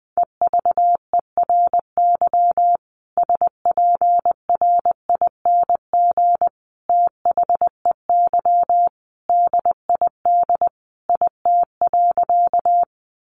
Morse code, heard as a keyed tone, about 20 wpm, 700 Hz. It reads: EVERY SPRING THEY DID IT.
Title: Sentences from Top 1200 Words 20wpm